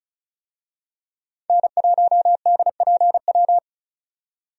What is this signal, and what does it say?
Morse code, heard as a keyed tone, about 36 wpm, 700 Hz. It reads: N1DPW
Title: USA Call Signs - 5 Characters - 35wpm